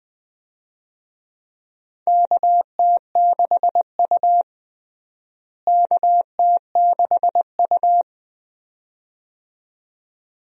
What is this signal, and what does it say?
Morse code, heard as a keyed tone, about 21 wpm, 700 Hz. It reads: KT6U KT6U
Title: USA Call Signs - 4 Characters - Repeated 2x - 20wpm